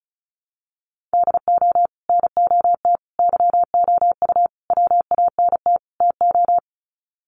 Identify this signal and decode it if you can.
Morse code, heard as a keyed tone, about 34 wpm, 700 Hz. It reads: DO NOT YOU WANT TO
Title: Sets of 5 Words Encore2 35wpm